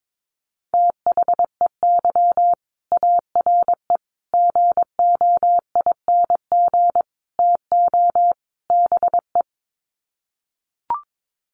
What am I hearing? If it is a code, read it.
Morse code, heard as a keyed tone, about 22 wpm, 700 Hz. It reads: THEY ARE GOING TO BE
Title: Sets of 5 Words Encore2 22wpm